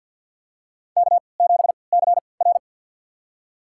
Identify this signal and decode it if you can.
Morse code, heard as a keyed tone, about 51 wpm, spaced as for 28 wpm, 700 Hz. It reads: K7CR
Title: USA Call Signs - 4 Characters - 3x Character Spacing - 50wpm